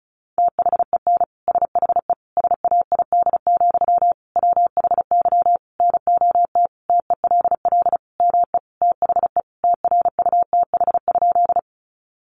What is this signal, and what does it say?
Morse code, heard as a keyed tone, about 34 wpm, 700 Hz. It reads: THEN SHE SAID, WHY NOT TELL ME THE TRUTH?
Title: Sentences from Top 800 Words 35wpm